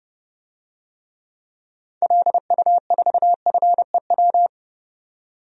Morse code, heard as a keyed tone, about 32 wpm, 700 Hz. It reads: LU4FEW
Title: DXCC-100 Call Signs - 6 Characters - 30wpm